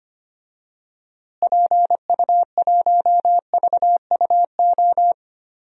Morse code, heard as a keyed tone, about 25 wpm, 700 Hz. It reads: PU1VUO
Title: DXCC-100 Call Signs - 6 Characters - Repeated 3x - 25wpm